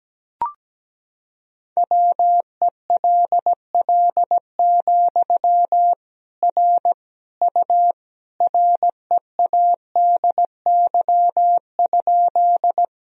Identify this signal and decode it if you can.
Morse code, heard as a keyed tone, about 17 wpm, 700 Hz. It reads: WELL, R U READY?
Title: Sentences from Top 400 Words and Abbreviations 17wpm